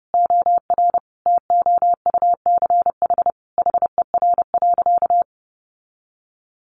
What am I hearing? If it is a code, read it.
Morse code, heard as a keyed tone, about 30 wpm, 700 Hz. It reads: OR TOUCH HER.
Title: Sentences from Top 1200 Words 30wpm